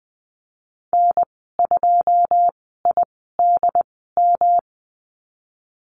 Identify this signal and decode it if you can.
Morse code, heard as a keyed tone, about 19 wpm, spaced as for 14 wpm, 700 Hz. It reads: N2IDM
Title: 179 Lesson 29g - Slash - Call Signs 20wpm